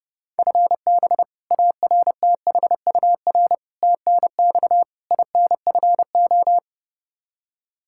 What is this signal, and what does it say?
Morse code, heard as a keyed tone, about 32 wpm, 700 Hz. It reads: FB ARTHUR TNX INFO
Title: Practice QSO - Exchange 3 - More Info - 30wpm